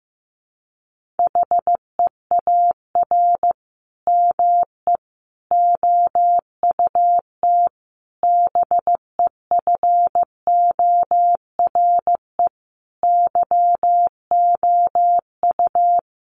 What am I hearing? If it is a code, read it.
Morse code, heard as a keyed tone, about 15 wpm, 700 Hz. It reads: HEAR ME OUT BEFORE YOU
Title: Sentences from Top 1400 Words 15wpm